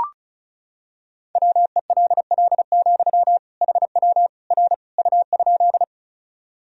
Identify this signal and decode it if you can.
Morse code, heard as a keyed tone, about 36 wpm, 700 Hz. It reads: WELL, HW R U?
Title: Sentences from Top 200 Words and Abbreviations 35wpm